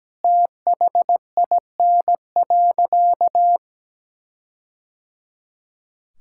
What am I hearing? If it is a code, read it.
Morse code, heard as a keyed tone, about 17 wpm, 700 Hz. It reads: THIN.
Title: Sentences from Top 1600 Words 17wpm